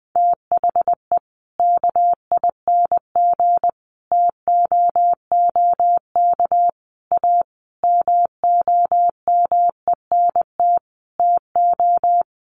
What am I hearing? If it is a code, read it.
Morse code, heard as a keyed tone, about 19 wpm, 700 Hz. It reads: THE KING TOOK A MOMENT TO
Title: Sentences from Top 1500 Words 20wpm